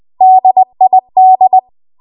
Morse code, heard as a keyed tone, about 20 wpm, 750 Hz. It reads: DID
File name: Archive / 20 WPM / 20WPM